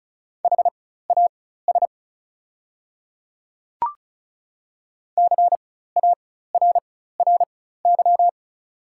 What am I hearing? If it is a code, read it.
Morse code, heard as a keyed tone, about 36 wpm, spaced as for 16 wpm, 700 Hz. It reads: HAS CARRY
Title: 4x Character Spacing - Top 300 Words - 35wpm